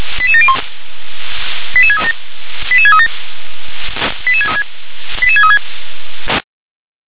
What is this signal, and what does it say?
Five-tone selective call (ZVEI), heard as quick runs of five tones, tones of about 70 ms. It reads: 88701 88637 88637 88636
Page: ZVEI